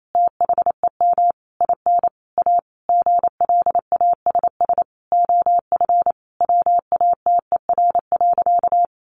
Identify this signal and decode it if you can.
Morse code, heard as a keyed tone, about 28 wpm, 700 Hz. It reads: THEM IN A GLASS OF WATER.
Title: Sentences from Top 1300 Words 28wpm